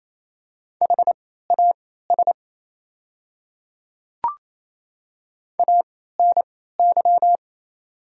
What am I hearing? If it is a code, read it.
Morse code, heard as a keyed tone, about 28 wpm, spaced as for 16 wpm, 690 Hz. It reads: HAS ANY
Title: 3x Character Spacing - Top 500 Words - 28wpm